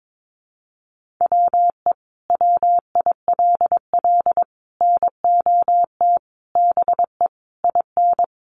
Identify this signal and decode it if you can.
Morse code, heard as a keyed tone, about 22 wpm, 700 Hz. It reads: WE WILL NOT BE IN
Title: Sentences from Top 1300 Words 22wpm